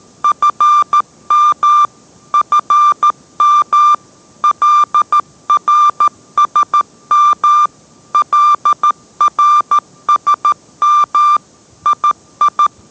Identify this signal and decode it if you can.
Morse code, heard as a keyed tone, about 13 wpm, 1190 Hz. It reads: FM FM LRSM LRSM II